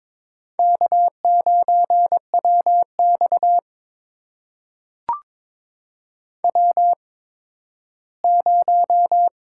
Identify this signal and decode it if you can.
Morse code, heard as a keyed tone, about 22 wpm, spaced as for 13 wpm, 700 Hz. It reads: K9WX W0
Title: USA Call Signs - 4 Characters - 8x Character Spacing - 22wpm